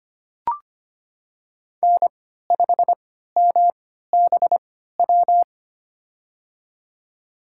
Morse code, heard as a keyed tone, about 25 wpm, spaced as for 14 wpm, 700 Hz. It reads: N5MBW
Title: USA Call Signs - 5 Characters - 3x Character Spacing - 25wpm